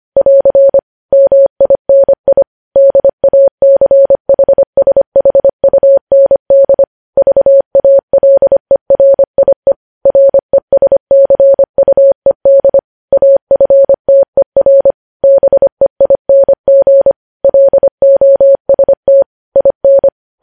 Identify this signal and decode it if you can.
Morse code, heard as a keyed tone, about 25 wpm, 550 Hz. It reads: <AR> MINI DACHSHUND VALERIE RESCUED AFTER BEING LOST IN
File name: podcast / News headlines-1x-25x25